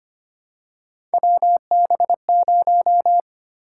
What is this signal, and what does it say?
Morse code, heard as a keyed tone, about 25 wpm, 700 Hz. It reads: WB0
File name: Three Letter-Number Combinations - 25wpm